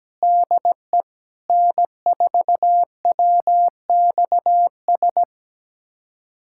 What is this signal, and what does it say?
Morse code, heard as a keyed tone, about 17 wpm, 700 Hz. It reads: DE N4WXS
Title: USA Callsign DE Callsign - 5 Characters - 17wpm